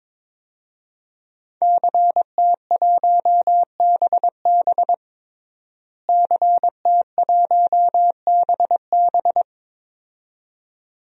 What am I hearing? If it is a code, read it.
Morse code, heard as a keyed tone, about 22 wpm, 700 Hz. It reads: CT1BB CT1BB
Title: DXCC-100 Call Signs - 5 Characters - Repeated 2x - 22wpm